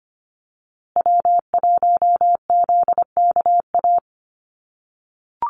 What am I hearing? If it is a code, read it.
Morse code, heard as a keyed tone, about 25 wpm, 700 Hz. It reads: W1ZKA
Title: USA Call Signs Repeated 3x 25wpm